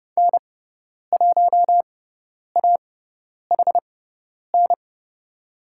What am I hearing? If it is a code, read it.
Morse code, heard as a keyed tone, about 30 wpm, spaced as for 10 wpm, 700 Hz. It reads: N1AHN
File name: Lesson 16g - Letter c - Call Signs 30wpm